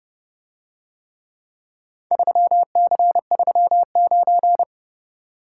Morse code, heard as a keyed tone, about 30 wpm, 700 Hz. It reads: 3C39